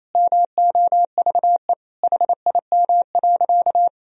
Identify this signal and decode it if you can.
Morse code, heard as a keyed tone, about 28 wpm, 700 Hz. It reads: MOVE HIM.
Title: Sentences from Top 1000 Words - Review Part 1 - 28wpm